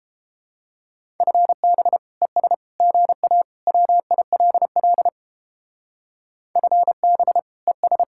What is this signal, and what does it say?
Morse code, heard as a keyed tone, about 31 wpm, 700 Hz. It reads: FB ES GA WILL FB ES